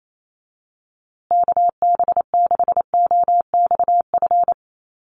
Morse code, heard as a keyed tone, about 28 wpm, 700 Hz. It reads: KB6OXF
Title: USA Call Signs Repeated 2x 28wpm